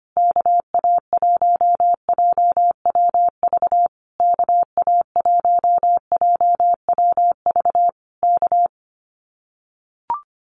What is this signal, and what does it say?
Morse code, heard as a keyed tone, about 25 wpm, 700 Hz. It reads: KA1JWV KA1JWV K
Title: Common QSO Phrases Encore 25wpm